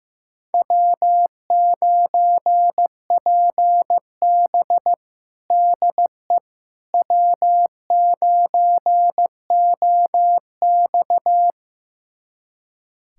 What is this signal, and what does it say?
Morse code, heard as a keyed tone, about 15 wpm, 700 Hz. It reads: W9PB DE W9OX
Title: USA Callsign DE Callsign - 4 Characters - 15wpm